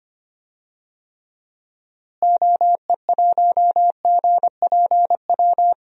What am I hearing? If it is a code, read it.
Morse code, heard as a keyed tone, about 25 wpm, 700 Hz. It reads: OE1GPW
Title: DXCC-100 Call Signs - 6 Characters - Repeated 3x - 25wpm